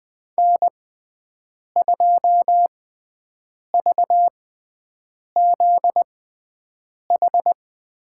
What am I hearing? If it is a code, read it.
Morse code, heard as a keyed tone, about 20 wpm, spaced as for 7 wpm, 700 Hz. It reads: N2VZH
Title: USA Call Signs - 6x Character Spacing - 20wpm